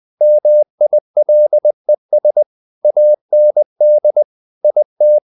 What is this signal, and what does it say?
Morse code, heard as a keyed tone, about 20 wpm, 600 Hz. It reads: MILES AND IT
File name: the old man and the sea part 2 20wpm